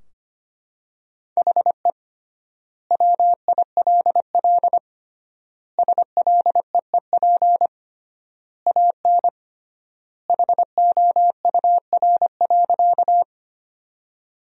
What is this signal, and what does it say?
Morse code, heard as a keyed tone, about 25 wpm, 700 Hz. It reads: HE WILL SLEEP AN HOUR.
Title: 3x Word-Spacing - Sentences from Top 500 Words - 25wpm